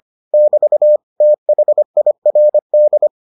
Morse code, heard as a keyed tone, about 25 wpm, 600 Hz. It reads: <BT> THIRD